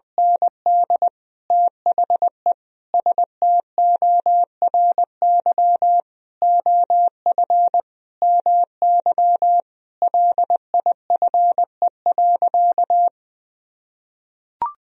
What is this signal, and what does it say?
Morse code, heard as a keyed tone, about 21 wpm, 700 Hz. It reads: ND THE STORY OF MY LIFE.
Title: Sentences from Top 1000 Words - Review Part 1 - 20wpm